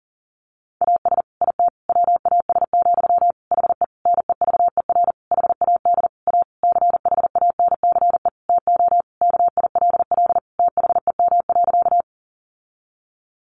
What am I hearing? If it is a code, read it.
Morse code, heard as a keyed tone, about 40 wpm, 700 Hz. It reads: AS IT WAS, HE NEVER HAD A CHANCE TO KILL THEM.
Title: Sentences from Top 1400 Words 40wpm